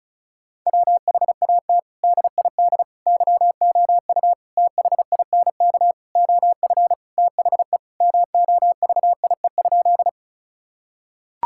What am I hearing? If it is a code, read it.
Morse code, heard as a keyed tone, about 36 wpm, 700 Hz. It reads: WHAT DID YOU THINK OF THE MOVIE?
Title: Sentences from Top 1000 Words - Review Part 5 - 35wpm